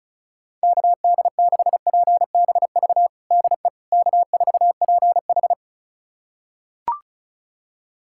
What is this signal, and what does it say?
Morse code, heard as a keyed tone, about 36 wpm, 700 Hz. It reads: KD6PBV DE K4PH
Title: USA Callsign DE Callsign 35wpm